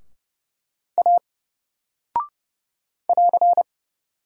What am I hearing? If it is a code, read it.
Morse code, heard as a keyed tone, about 29 wpm, 700 Hz. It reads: A <AR>
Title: Single Character (Letter:Number:Punctuation:Prosign) 30wpm